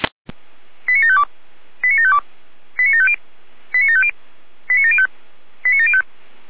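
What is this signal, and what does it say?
Five-tone selective call (ZVEI), heard as quick runs of five tones, tones of about 70 ms. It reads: 89752 89752 89750 89750 89705 89705
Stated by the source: ZVEI five-tone selective call 89518, received by SDR.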